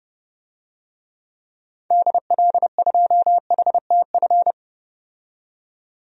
Morse code, heard as a keyed tone, about 32 wpm, 700 Hz. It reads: DL2HTF
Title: DXCC-100 Call Signs - 6 Characters - 30wpm